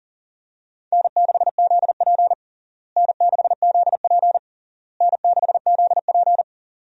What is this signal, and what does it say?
Morse code, heard as a keyed tone, about 37 wpm, 700 Hz. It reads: N6ZP N6ZP N6ZP
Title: USA Call Signs - 4 Characters - Repeated 3x - 40wpm